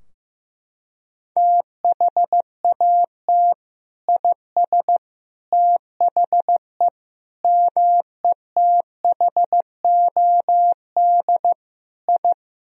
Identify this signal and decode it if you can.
Morse code, heard as a keyed tone, about 16 wpm, 700 Hz. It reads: THAT IS THE METHOD I
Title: Sentences from Top 1000 Words - Review Part 3 - 15wpm